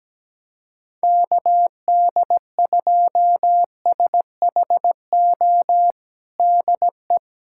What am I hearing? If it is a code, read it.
Morse code, heard as a keyed tone, about 17 wpm, 700 Hz. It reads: KD2SHO DE
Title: USA Callsign DE Callsign - 6 Characters - 17wpm